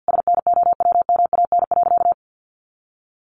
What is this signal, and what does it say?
Morse code, heard as a keyed tone, about 50 wpm, 700 Hz. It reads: SNOWMAN.
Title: Easy Phrases 50wpm